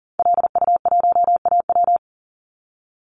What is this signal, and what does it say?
Morse code, heard as a keyed tone, about 38 wpm, 700 Hz. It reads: LU1AW
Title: Call Signs - 40wpm